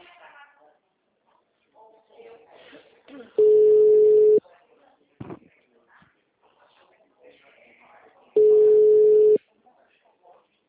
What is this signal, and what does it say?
Ringback tone (Europe, 425 Hz), with speech.
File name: out- -2018- - - .412